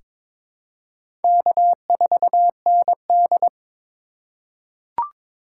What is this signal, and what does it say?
Morse code, heard as a keyed tone, about 22 wpm, 700 Hz. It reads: K4ND
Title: USA Call Signs - 4 Characters - 4x Character Spacing - 22wpm